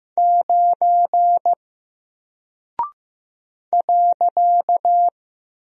Morse code, heard as a keyed tone, about 15 wpm, 700 Hz. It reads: 9 .
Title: Single Character (Letter:Number:Punctuation:Prosign) Alt 15wpm